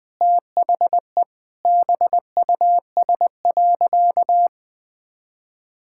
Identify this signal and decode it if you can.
Morse code, heard as a keyed tone, about 20 wpm, 700 Hz. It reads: THE BUS.
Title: Sentences from Top 1000 Words - Review Part 6 - 20wpm